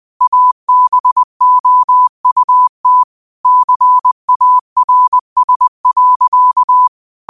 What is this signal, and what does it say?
Morse code, heard as a keyed tone, about 20 wpm, 1000 Hz. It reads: ABOUT CARS.
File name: qotdfiles / 20wpm